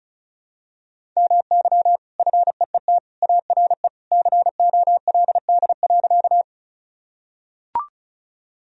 Morse code, heard as a keyed tone, about 34 wpm, 700 Hz. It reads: MY FEET ARE COLD.
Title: Sentences from Top 1000 Words - Review Part 1 - 35wpm